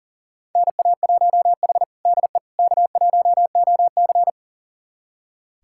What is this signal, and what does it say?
Morse code, heard as a keyed tone, about 40 wpm, 700 Hz. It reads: NA1H DE K1OC
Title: USA Callsign DE Callsign - 4 Characters - 40wpm